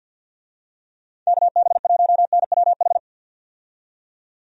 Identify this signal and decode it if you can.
Morse code, heard as a keyed tone, about 51 wpm, 700 Hz. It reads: KB1NWH
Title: USA Call Signs - 6 Characters - 50wpm